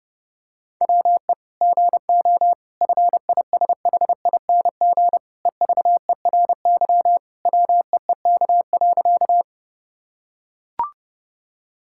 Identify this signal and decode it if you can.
Morse code, heard as a keyed tone, about 30 wpm, 700 Hz. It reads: WE GO FISHING EVERY WEEK.
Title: Sentences from Top 1000 Words - Review Part 5 - 30wpm